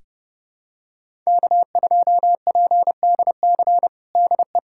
Morse code, heard as a keyed tone, about 31 wpm, 700 Hz. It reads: K2PDC DE
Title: USA Callsign DE Callsign - 5 Characters - 30wpm